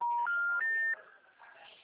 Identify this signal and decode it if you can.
Special information tone.
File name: out- -2017- - - .54420